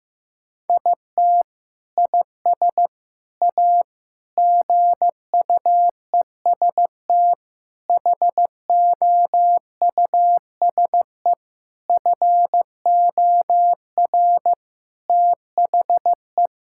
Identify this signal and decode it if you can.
Morse code, heard as a keyed tone, about 15 wpm, 700 Hz. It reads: IT IS A GUEST HOUSE FOR THE
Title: Sentences from Top 1600 Words 15wpm